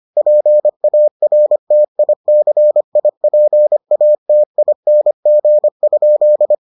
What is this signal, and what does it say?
Morse code, heard as a keyed tone, about 25 wpm, 600 Hz. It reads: PARTICIPATING?